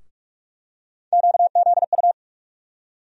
Morse code, heard as a keyed tone, about 46 wpm, 700 Hz. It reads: QZU
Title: USA Call Sign Postfixes - 3 Letters 45wpm